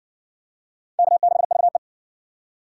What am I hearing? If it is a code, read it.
Morse code, heard as a keyed tone, about 60 wpm, 700 Hz. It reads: K6VE